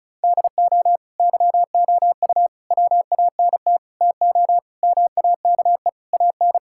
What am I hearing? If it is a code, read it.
Morse code, heard as a keyed tone, about 34 wpm, 700 Hz. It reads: DO YOU WANT TO MAKE AN
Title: Sentences from Top 1000 Words - Review Part 6 - 35wpm